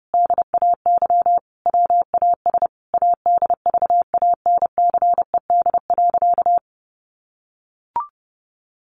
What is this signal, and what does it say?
Morse code, heard as a keyed tone, about 30 wpm, 700 Hz. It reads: DAY WAS ADVANCED.
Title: Sentences from Top 1200 Words 30wpm